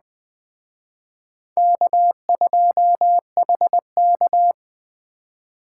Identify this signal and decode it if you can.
Morse code, heard as a keyed tone, about 23 wpm, 700 Hz. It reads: K2HK
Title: Speed Racing - USA 1x2 Call Signs 20wpm